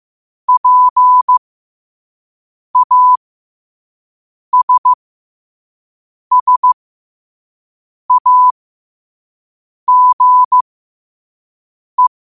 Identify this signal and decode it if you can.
Morse code, heard as a keyed tone, about 15 wpm, spaced as for 5 wpm, 1000 Hz. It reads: PASSAGE
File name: qotdfiles / 5wpm